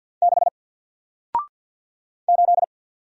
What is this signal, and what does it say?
Morse code, heard as a keyed tone, about 53 wpm, 700 Hz. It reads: 6 8
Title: Single Letter-Number Alt 50wpm